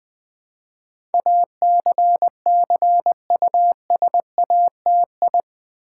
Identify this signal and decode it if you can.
Morse code, heard as a keyed tone, about 20 wpm, 700 Hz. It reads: ACCUSATI
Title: Build A Word - 4 Letters At A Time - 10-Letter Words - 20wpm